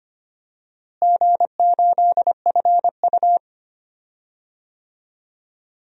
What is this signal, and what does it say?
Morse code, heard as a keyed tone, about 25 wpm, 700 Hz. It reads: G8FU
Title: DXCC-100 Call Signs - 4 Characters - Repeated 3x - 25wpm